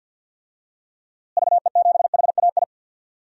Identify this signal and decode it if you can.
Morse code, heard as a keyed tone, about 49 wpm, 700 Hz. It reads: VE7HAI
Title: DXCC-100 Call Signs - 6 Characters - Repeated 2x - 50wpm